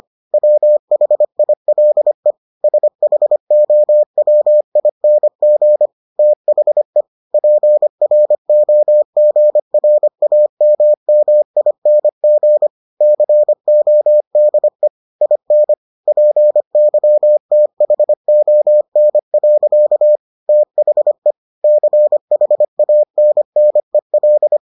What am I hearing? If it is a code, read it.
Morse code, heard as a keyed tone, about 25 wpm, 600 Hz. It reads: WHILE SHOWING THE PROGRAMMING CODE IN PYTHON. THE CHANNEL